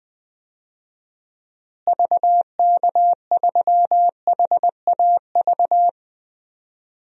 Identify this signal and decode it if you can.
Morse code, heard as a keyed tone, about 20 wpm, 690 Hz. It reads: VK3HAV
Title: DXCC-100 Call Signs - 6 Characters - Repeated 2x - 20wpm